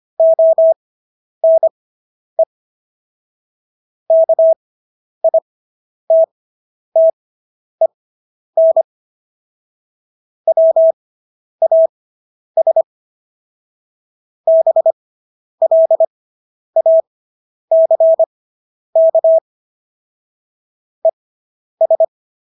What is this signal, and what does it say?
Morse code, heard as a keyed tone, about 25 wpm, spaced as for 10 wpm, 650 Hz. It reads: ONE KITTEN WAS BLACK ES